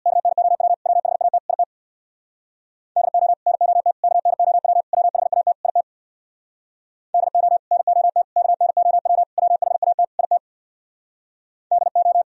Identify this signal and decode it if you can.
Morse code, heard as a keyed tone, about 60 wpm, 700 Hz. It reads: KNOW WHAT IT DO NOT KNOW WHAT IT DO NOT KNOW WHAT IT DO
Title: Speed-Racing - Sets of 5 Words 50wpm